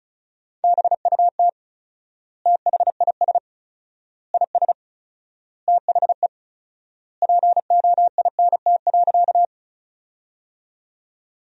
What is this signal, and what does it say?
Morse code, heard as a keyed tone, about 34 wpm, 700 Hz. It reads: BUT THIS IS THE POINT.
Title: 4x Word-Spacing - Sentences from Top 200 Words - 35wpm